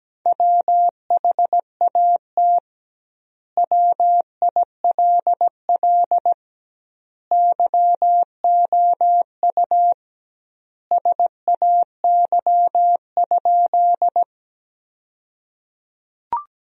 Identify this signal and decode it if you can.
Morse code, heard as a keyed tone, about 17 wpm, 700 Hz. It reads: WHAT WILL YOU SAY?
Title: 2x Word-Spacing - Sentences from Top 200 Words - 17wpm